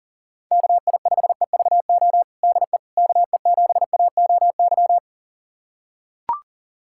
Morse code, heard as a keyed tone, about 37 wpm, 700 Hz. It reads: KI5EVO DE KE7AOY
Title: USA Callsign DE Callsign 40wpm